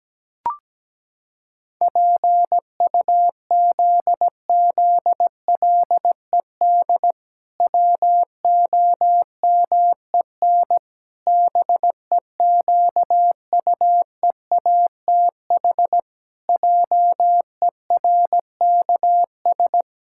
Morse code, heard as a keyed tone, about 17 wpm, 700 Hz. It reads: PUZZLED WOMEN BEQUEATH JERKS
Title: Pangram Challenge v2 - 17wpm